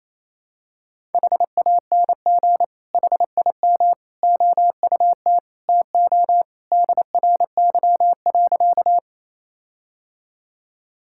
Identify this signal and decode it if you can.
Morse code, heard as a keyed tone, about 28 wpm, 700 Hz. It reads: HANG HIM OUT TO DRY.
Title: Sentences from Top 1600 Words 28wpm